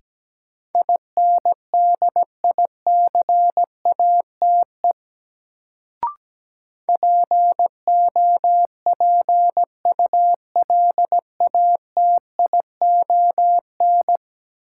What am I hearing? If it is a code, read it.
Morse code, heard as a keyed tone, about 17 wpm, 700 Hz. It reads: INDICATE POPULATION
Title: Sentences from Top 1300 Words - New Word Drill Alt - 17wpm